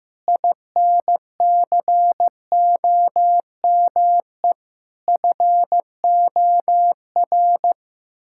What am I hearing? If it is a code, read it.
Morse code, heard as a keyed tone, about 15 wpm, 700 Hz. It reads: INCOME FOR
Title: Sentences from Top 1000 Words - Review Part 5 - 15wpm